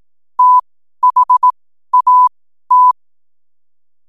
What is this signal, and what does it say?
Morse code, heard as a keyed tone, about 18 wpm, spaced as for 12 wpm, 1000 Hz. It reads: THAT
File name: n7net / 12wpm